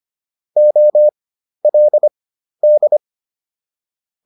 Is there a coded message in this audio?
Yes: Morse code from 0.6 to 3.0 s, heard as a keyed tone, about 25 wpm, spaced as for 12 wpm, 600 Hz. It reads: OLD